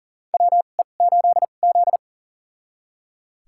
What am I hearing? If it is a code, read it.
Morse code, heard as a keyed tone, about 40 wpm, spaced as for 29 wpm, 700 Hz. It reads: WE8Z
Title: USA Call Signs - 4 Characters - 2x Character Spacing - 40wpm